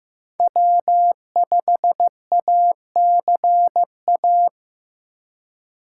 Morse code, heard as a keyed tone, about 15 wpm, 700 Hz. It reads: W5ACA
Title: USA Call Signs - 5 Characters - 6x Character Spacing - 15wpm